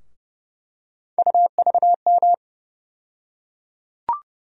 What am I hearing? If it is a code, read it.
Morse code, heard as a keyed tone, about 30 wpm, 700 Hz. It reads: UVM
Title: USA Call Sign Postfixes - 3 Letters 30wpm